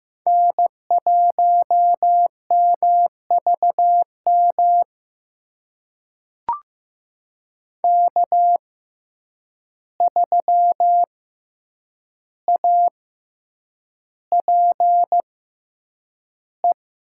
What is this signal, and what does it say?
Morse code, heard as a keyed tone, about 15 wpm, spaced as for 8 wpm, 700 Hz. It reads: N1MVM K3APE
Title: USA Call Signs - 5 Characters - 6x Character Spacing - 15wpm